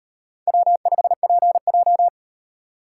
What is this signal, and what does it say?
Morse code, heard as a keyed tone, about 38 wpm, 700 Hz. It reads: W5PJ